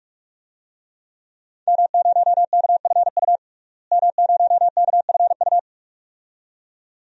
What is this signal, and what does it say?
Morse code, heard as a keyed tone, about 42 wpm, 700 Hz. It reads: M0KFU M0KFU
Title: DXCC-100 Call Signs - 5 Characters - Repeated 2x - 45wpm